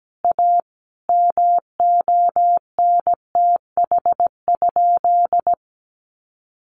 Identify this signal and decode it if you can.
Morse code, heard as a keyed tone, about 17 wpm, 700 Hz. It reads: A MONTH?
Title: Sentences from Top 1200 Words 17wpm